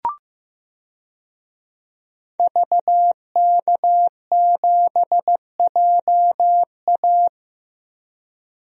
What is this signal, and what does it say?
Morse code, heard as a keyed tone, about 15 wpm, 700 Hz. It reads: VK7JA
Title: DXCC-100 Call Signs - 5 Characters - 15wpm